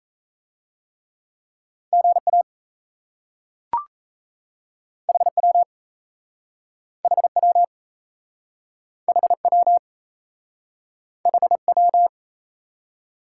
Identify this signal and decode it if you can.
Morse code, heard as a keyed tone, about 37 wpm, 700 Hz. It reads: GA HW HW HW HW